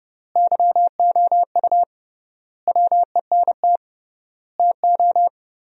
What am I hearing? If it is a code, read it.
Morse code, heard as a keyed tone, about 30 wpm, 700 Hz. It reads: YOU WENT TO
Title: 3x Word-Spacing - Sentences from Top 400 Words - 30wpm